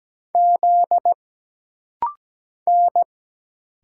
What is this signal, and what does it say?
Morse code, heard as a keyed tone, about 17 wpm, 700 Hz. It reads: Z N
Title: Single Letter-Number - Rapid-Fire 17wpm